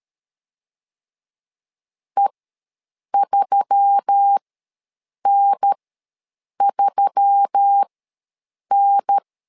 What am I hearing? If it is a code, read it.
Morse code, heard as a keyed tone, about 13 wpm, spaced as for 7 wpm, 800 Hz. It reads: E3N3N
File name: LEZIONI / 4 Q1S3 ws 12wpm 2038